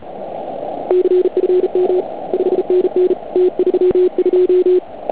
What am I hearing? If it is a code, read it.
Morse code, heard as a keyed tone, about 36 wpm, 360 Hz. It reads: CFM 5NN T32
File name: Archiv / CUC149 / OK1IF CUC149